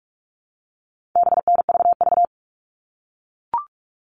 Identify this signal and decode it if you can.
Morse code, heard as a keyed tone, about 44 wpm, 700 Hz. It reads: BNVV
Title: All Cut Numbers - 4-Digit 45wpm